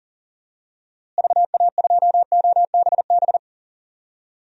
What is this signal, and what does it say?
Morse code, heard as a keyed tone, about 40 wpm, 700 Hz. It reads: VA2OBB
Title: DXCC-100 Call Signs - 6 Characters - Repeated 3x - 40wpm